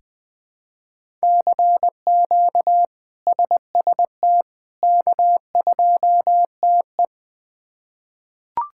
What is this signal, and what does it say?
Morse code, heard as a keyed tone, about 20 wpm, 700 Hz. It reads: CQ SST K2TE
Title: K1USN SST QSO Practice - 20wpm